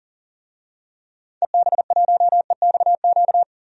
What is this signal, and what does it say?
Morse code, heard as a keyed tone, about 40 wpm, 700 Hz. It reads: EB1EXQ
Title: DXCC-100 Call Signs - 6 Characters - 40wpm